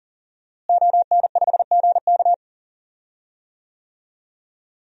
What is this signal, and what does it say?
Morse code, heard as a keyed tone, about 40 wpm, 700 Hz. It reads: ON5GK
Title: DXCC-100 Call Signs - 5 Characters - Repeated 3x - 40wpm